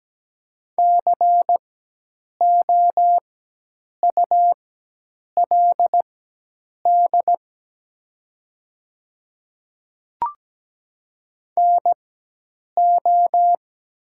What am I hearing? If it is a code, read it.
Morse code, heard as a keyed tone, about 17 wpm, spaced as for 8 wpm, 700 Hz. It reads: COULD NO
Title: 4x Character Spacing - Top 100 Words - 17wpm